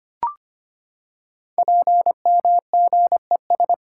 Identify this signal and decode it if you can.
Morse code, heard as a keyed tone, about 25 wpm, 700 Hz. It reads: PMGES
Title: Code Groups - 5 Letters - 25wpm